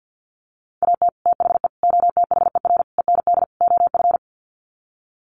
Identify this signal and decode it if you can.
Morse code, heard as a keyed tone, about 50 wpm, 700 Hz. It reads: AT THE OTHER END OF
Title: Sets of 5 Words 50wpm